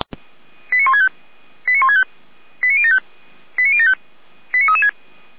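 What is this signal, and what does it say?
Five-tone selective call (ZVEI), heard as quick runs of five tones, tones of about 70 ms. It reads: 89165 89165 89075 89075 89336
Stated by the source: ZVEI five-tone call received on SDR by feuerwehr-orlamuende.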